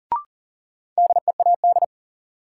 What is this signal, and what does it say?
Morse code, heard as a keyed tone, about 37 wpm, 700 Hz. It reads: DEAD
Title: Sentences from Top 1500 Words - New Word Drill Alt - 40wpm